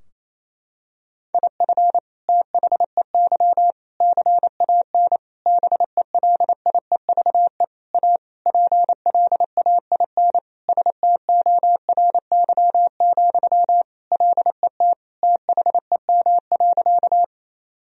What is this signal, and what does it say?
Morse code, heard as a keyed tone, about 28 wpm, 700 Hz. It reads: IF THEY CAN BELIEVE A PLAIN STORY, LET THEM.
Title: Sentences from Top 1000 Words - Review Part 2 - 28wpm